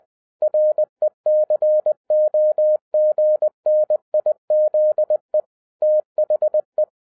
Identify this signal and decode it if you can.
Morse code, heard as a keyed tone, about 20 wpm, 610 Hz. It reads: RECOGNIZE THE
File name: 2011-10-mp3 / n13-TIPS FROM MY 34 YEARS OF EXPERIENCE 20wpm